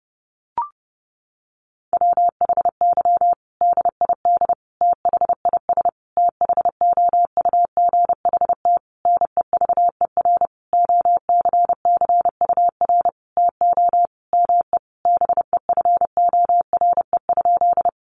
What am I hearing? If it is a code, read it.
Morse code, heard as a keyed tone, about 30 wpm, 700 Hz. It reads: WHY DID THIS THOUGHT NEVER OCCUR TO ME BEFORE?
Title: Sentences from Top 900 Words 30wpm